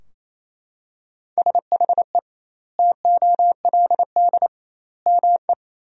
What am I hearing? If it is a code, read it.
Morse code, heard as a keyed tone, about 28 wpm, 700 Hz. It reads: SHE TOLD ME
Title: 2x Word-Spacing - Sentences from Top 400 Words - 28wpm